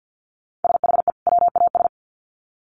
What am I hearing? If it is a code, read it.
Morse code, heard as a keyed tone, about 50 wpm, 700 Hz. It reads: SHE WAS
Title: Sets of 2 Words 50wpm